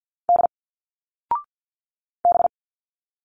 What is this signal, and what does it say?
Morse code, heard as a keyed tone, about 48 wpm, 700 Hz. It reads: D B
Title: Single Letters 50wpm